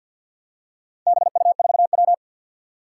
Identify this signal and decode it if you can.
Morse code, heard as a keyed tone, about 51 wpm, 700 Hz. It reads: BU4W
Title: Four Challenging Letter-Number Combinations 50wpm